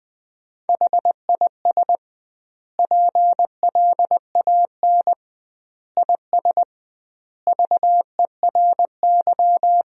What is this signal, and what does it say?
Morse code, heard as a keyed tone, about 21 wpm, 700 Hz. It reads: HIS PLAN IS VERY
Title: 2x Word-Spacing - Sentences from Top 500 Words - 20wpm